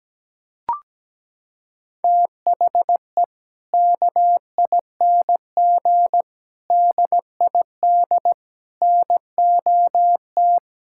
Morse code, heard as a keyed tone, about 17 wpm, 700 Hz. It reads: THE KING DID NOT
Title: Sentences from Top 1000 Words - Review Part 5 - 17wpm